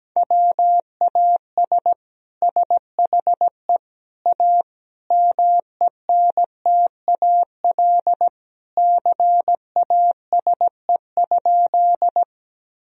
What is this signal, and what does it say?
Morse code, heard as a keyed tone, about 17 wpm, 700 Hz. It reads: WAS SHE A MENTAL CASE?
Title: Sentences from Top 1000 Words - Review Part 4 - 17wpm